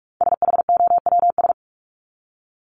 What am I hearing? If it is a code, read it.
Morse code, heard as a keyed tone, about 44 wpm, 700 Hz. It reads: SHOWS
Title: ES:S Suffix - Top 100 Words with Base Words 45wpm